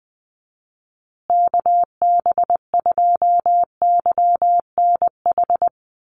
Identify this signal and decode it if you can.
Morse code, heard as a keyed tone, about 20 wpm, 700 Hz. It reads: KB2YNH
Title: USA Call Signs Repeated 3x 20wpm